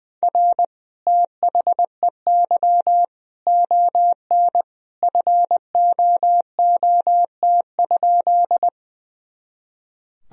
Morse code, heard as a keyed tone, about 21 wpm, 700 Hz. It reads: R THEY ON FOOT?
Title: Sentences from Top 500 Words and Abbreviations 20wpm